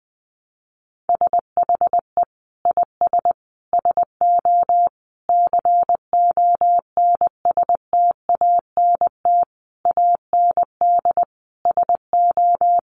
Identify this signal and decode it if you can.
Morse code, heard as a keyed tone, about 20 wpm, 700 Hz. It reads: SHE IS SO CONSTANT AND SO
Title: Sentences from Top 1500 Words 20wpm